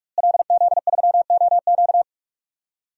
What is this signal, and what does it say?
Morse code, heard as a keyed tone, about 41 wpm, 700 Hz. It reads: LZ3OQ
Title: DXCC-100 Call Signs - 5 Characters - Repeated 3x - 45wpm